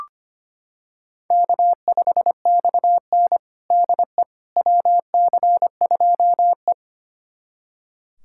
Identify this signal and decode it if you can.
Morse code, heard as a keyed tone, about 25 wpm, 700 Hz. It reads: K5XN DE WC2E
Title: USA Callsign DE Callsign - 4 Characters - 25wpm